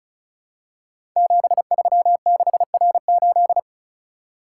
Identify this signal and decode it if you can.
Morse code, heard as a keyed tone, about 36 wpm, 700 Hz. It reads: 736R8
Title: Code Groups - 5 Letter:Numbers - 35wpm